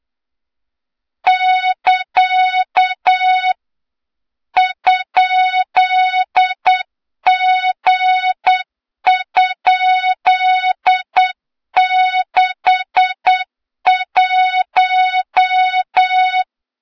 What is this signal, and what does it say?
Morse code, heard as a keyed tone, about 8 wpm, 3700 Hz, with noise. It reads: <KA> ?G?61
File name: Morze-HA2MN / hang / 23 GYAKORLAT